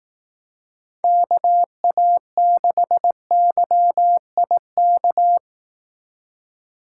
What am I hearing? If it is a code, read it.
Morse code, heard as a keyed tone, about 18 wpm, 700 Hz. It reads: KA6YIK
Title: Speed Racing - USA 2x3 Call Signs 15wpm